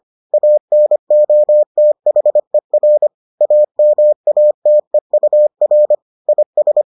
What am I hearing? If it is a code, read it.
Morse code, heard as a keyed tone, about 25 wpm, 600 Hz. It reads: ANOTHER AMATEUR IS